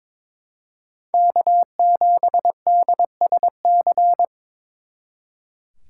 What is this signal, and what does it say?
Morse code, heard as a keyed tone, about 26 wpm, 700 Hz. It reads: K7DSC
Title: Speed Racing - USA Call Signs 22wpm